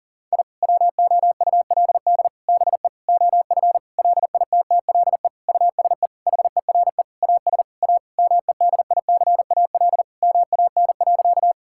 Morse code, heard as a keyed tone, about 40 wpm, 700 Hz. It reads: I WOULD BE OF LITTLE USE HERE AS A MEDICAL MAN.
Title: Sentences from Top 1000 Words - Review Part 5 - 40wpm